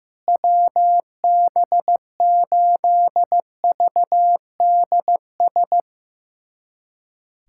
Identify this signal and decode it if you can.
Morse code, heard as a keyed tone, about 15 wpm, 700 Hz. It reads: WB8VDS
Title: USA Callsign DE Callsign - 6 Characters - 15wpm